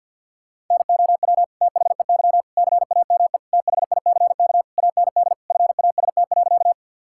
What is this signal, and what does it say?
Morse code, heard as a keyed tone, about 50 wpm, 700 Hz. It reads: NOW THEY CAME THICK AND FAST.
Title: Sentences from Top 1600 Words 50wpm